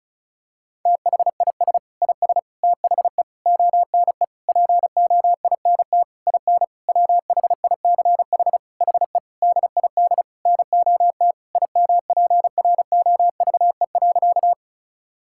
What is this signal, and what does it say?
Morse code, heard as a keyed tone, about 36 wpm, 700 Hz. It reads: THIS IS THE ONE POINT IN WHICH HE DID NOT IMPROVE.
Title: Sentences from Top 1000 Words - Review Part 5 - 35wpm